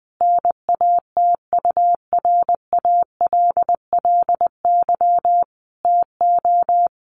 Morse code, heard as a keyed tone, about 20 wpm, 700 Hz. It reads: NATURALLY TO
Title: Sentences from Top 1300 Words 20wpm